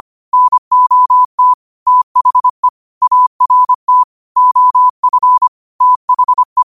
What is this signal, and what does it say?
Morse code, heard as a keyed tone, about 25 wpm, 1000 Hz. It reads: NOT THE ART OF THE
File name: qotdfiles / 25wpm